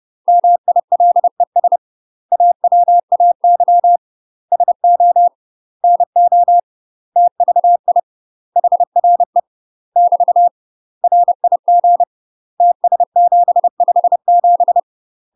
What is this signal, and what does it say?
Morse code, heard as a keyed tone, about 30 wpm, 700 Hz. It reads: MILES AWAY SO NO TVI HRE <BT> RIG TS757
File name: UK MASTER QSOs TRAINING / 30WPM / UK MASTER QSOs 0000